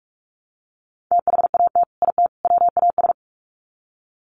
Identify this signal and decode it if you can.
Morse code, heard as a keyed tone, about 44 wpm, 700 Hz. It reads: THAT IT WAS
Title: Sets of 3 Words 45wpm